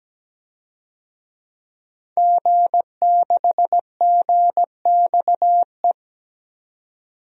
Morse code, heard as a keyed tone, about 17 wpm, 700 Hz. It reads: G6GXE
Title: DXCC-100 Call Signs - 5 Characters - 17wpm